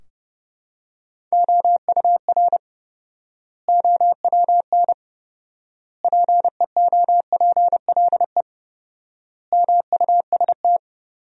Morse code, heard as a keyed tone, about 32 wpm, 700 Hz. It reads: OUR OWN PEOPLE MUST
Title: 4x Word-Spacing - Sentences from Top 300 Words - 30wpm